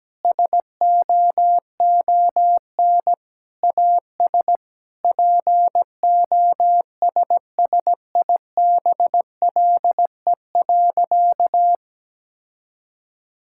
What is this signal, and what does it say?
Morse code, heard as a keyed tone, about 17 wpm, 700 Hz. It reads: SOON AS POSSIBLE.
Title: Sentences from Top 1000 Words - Review Part 5 - 17wpm